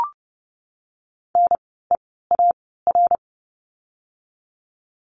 Morse code, heard as a keyed tone, about 30 wpm, spaced as for 17 wpm, 700 Hz. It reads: NEAR
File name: Lesson 16f - Letter c - Words 30wpm